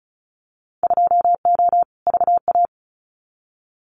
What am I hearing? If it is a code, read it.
Morse code, heard as a keyed tone, about 36 wpm, 700 Hz. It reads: 2O VA
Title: Winter Field Day Exchanges 35wpm